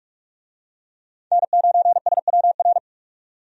Morse code, heard as a keyed tone, about 44 wpm, 700 Hz. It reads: N9SWR
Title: USA Call Signs - 5 Characters - 45wpm